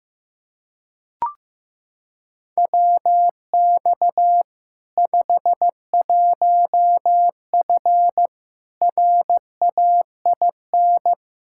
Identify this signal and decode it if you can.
Morse code, heard as a keyed tone, about 16 wpm, 700 Hz. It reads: WX 51F RAIN
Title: K1USN SST QSO Practice 15wpm